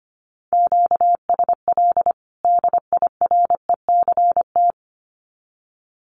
Morse code, heard as a keyed tone, about 25 wpm, 700 Hz. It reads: QSL DIRECT
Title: Common QSO Phrases 25wpm